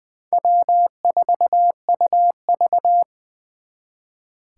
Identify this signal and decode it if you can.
Morse code, heard as a keyed tone, about 23 wpm, 700 Hz. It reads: W4UV
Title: Speed Racing - USA 1x2 Call Signs 20wpm